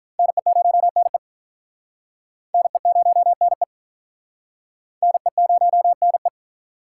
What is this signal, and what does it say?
Morse code, heard as a keyed tone, about 46 wpm, 700 Hz. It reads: NE0NE NE0NE NE0NE